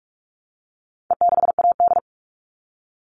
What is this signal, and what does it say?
Morse code, heard as a keyed tone, about 44 wpm, 700 Hz. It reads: E6AD
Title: All Cut Numbers - 4-Digit 45wpm